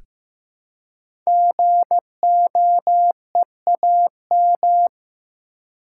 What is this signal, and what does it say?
Morse code, heard as a keyed tone, about 15 wpm, 700 Hz. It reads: GOEAM
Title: Code Groups - 5 Letters - 15wpm